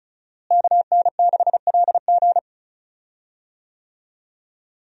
Morse code, heard as a keyed tone, about 36 wpm, 700 Hz. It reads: KN6LG
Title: USA Call Signs - 5 Characters - Repeated 2x - 35wpm